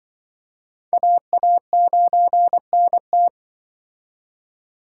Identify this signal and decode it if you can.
Morse code, heard as a keyed tone, about 25 wpm, 700 Hz. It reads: AA9NT
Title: Speed Racing - USA 2x2 Call Signs 20wpm